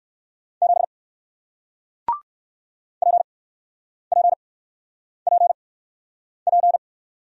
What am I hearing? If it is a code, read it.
Morse code, heard as a keyed tone, about 61 wpm, spaced as for 9 wpm, 700 Hz. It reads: 7 PPPP